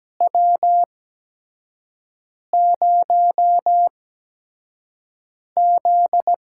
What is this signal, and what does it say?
Morse code, heard as a keyed tone, about 17 wpm, spaced as for 5 wpm, 700 Hz. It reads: W0Z
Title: USA Call Signs - 8x Character Spacing - 17wpm